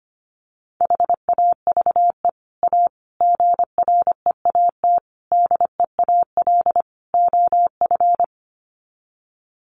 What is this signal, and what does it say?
Morse code, heard as a keyed tone, about 25 wpm, 700 Hz. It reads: HAVE A GREAT DEAL OF
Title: Sets of 5 Words Encore2 25wpm